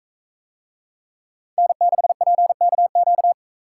Morse code, heard as a keyed tone, about 41 wpm, 700 Hz. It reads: N6PKQ